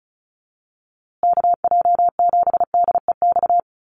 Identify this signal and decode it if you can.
Morse code, heard as a keyed tone, about 34 wpm, 700 Hz. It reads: KJ7DEX
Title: USA Call Signs Repeated 3x 35wpm